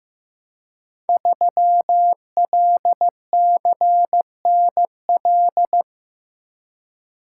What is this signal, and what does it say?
Morse code, heard as a keyed tone, about 15 wpm, 700 Hz. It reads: 3LCNL
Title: Code Groups - 5 Letter:Numbers - 15wpm